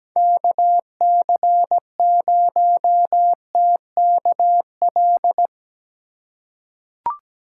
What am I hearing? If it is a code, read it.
Morse code, heard as a keyed tone, about 17 wpm, 700 Hz. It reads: KC0TKL
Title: USA Call Signs - 6 Characters - 8x Character Spacing - 17wpm